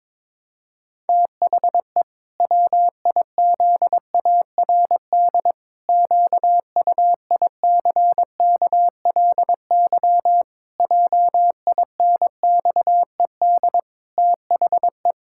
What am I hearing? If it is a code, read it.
Morse code, heard as a keyed tone, about 22 wpm, 700 Hz. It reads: THE WIZARD QUICKLY JINXED THE
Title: Pangram Challenge v2 - 22wpm